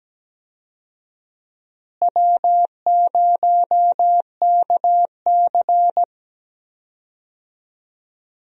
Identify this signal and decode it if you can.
Morse code, heard as a keyed tone, about 17 wpm, 700 Hz. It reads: W0KC
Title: USA Call Signs - 4 Characters - Repeated 3x - 17wpm